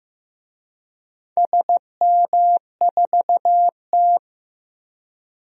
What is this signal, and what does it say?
Morse code, heard as a keyed tone, about 15 wpm, 700 Hz. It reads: SM4T
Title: DXCC-100 Call Signs - 4 Characters - Repeated 3x - 15wpm